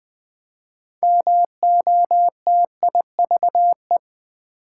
Morse code, heard as a keyed tone, about 20 wpm, 700 Hz. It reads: MOTIVE
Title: Build A Word - 2 Letters At A Time - 6-Letter Words - 20wpm